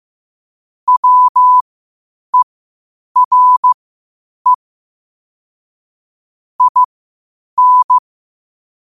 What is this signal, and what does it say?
Morse code, heard as a keyed tone, about 15 wpm, spaced as for 8 wpm, 1000 Hz. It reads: WERE IN
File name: qotdfiles / 8wpm